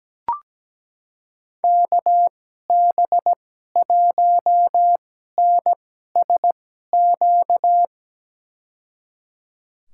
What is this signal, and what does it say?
Morse code, heard as a keyed tone, about 17 wpm, spaced as for 12 wpm, 700 Hz. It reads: KB1NSQ
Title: USA Call Signs - 6 Characters - 2x Character Spacing - 17wpm